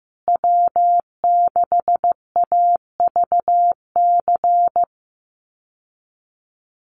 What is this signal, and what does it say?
Morse code, heard as a keyed tone, about 15 wpm, 700 Hz. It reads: W6AVC
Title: USA Call Signs Repeated 3x 15wpm